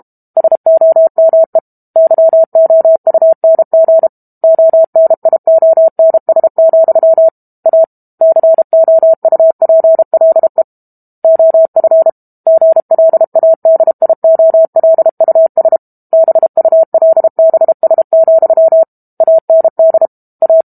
Morse code, heard as a keyed tone, about 32 wpm, 650 Hz. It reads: SOME YOUNG ONIONS, A COUPLE OF GLADIOLUS BULBS, AND A
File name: Book Two / 06-Chapter 6 - 32 WPM